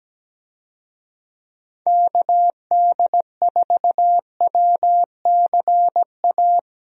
Morse code, heard as a keyed tone, about 17 wpm, 700 Hz. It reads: KD4WCA
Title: USA Call Signs - 6 Characters - Repeated 2x - 17wpm